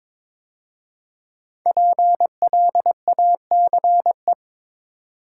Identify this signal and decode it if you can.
Morse code, heard as a keyed tone, about 22 wpm, 700 Hz. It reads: PLACE
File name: 5 Letter Words - First 3 Letters Given - 22wpm